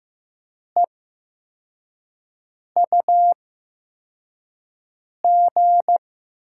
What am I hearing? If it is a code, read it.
Morse code, heard as a keyed tone, about 15 wpm, spaced as for 4 wpm, 700 Hz. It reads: EUG
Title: USA Call Signs - 8x Character Spacing - 15wpm